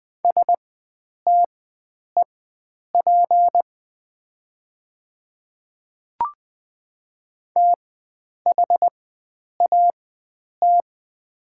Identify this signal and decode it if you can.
Morse code, heard as a keyed tone, about 20 wpm, spaced as for 9 wpm, 700 Hz. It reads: STEP THAT
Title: 4x Character Spacing - Top 500 Words - 20wpm